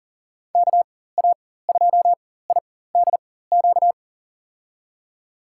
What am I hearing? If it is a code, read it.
Morse code, heard as a keyed tone, about 40 wpm, spaced as for 19 wpm, 700 Hz. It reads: KA2IDQ
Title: USA Call Signs - 6 Characters - 4x Character Spacing - 40wpm